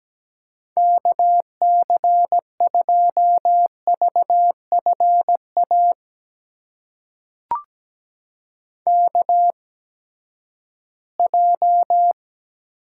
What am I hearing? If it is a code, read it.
Morse code, heard as a keyed tone, about 17 wpm, spaced as for 12 wpm, 700 Hz. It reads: KC2VFA KJ
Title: USA Call Signs - 6 Characters - 8x Character Spacing - 17wpm